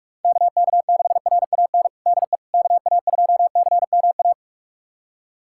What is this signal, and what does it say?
Morse code, heard as a keyed tone, about 46 wpm, 700 Hz. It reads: KK6RAN DE KA2CMA
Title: USA Callsign DE Callsign - 6 Characters - 45wpm